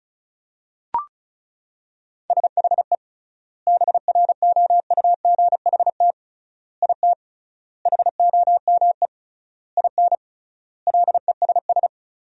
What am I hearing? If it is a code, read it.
Morse code, heard as a keyed tone, about 34 wpm, 700 Hz. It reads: SHE BROUGHT IT HOME IN LESS
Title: 3x Word-Spacing - Sentences from Top 500 Words - 35wpm